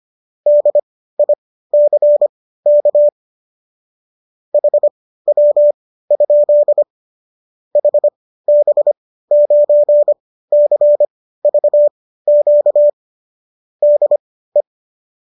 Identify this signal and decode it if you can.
Morse code, heard as a keyed tone, about 25 wpm, spaced as for 15 wpm, 590 Hz. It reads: DICK HW? HB9CVQ DE